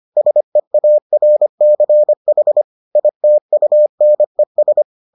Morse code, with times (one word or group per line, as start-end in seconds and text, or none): 0.2-2.6 SEARCH
3.0-4.8 ITUNES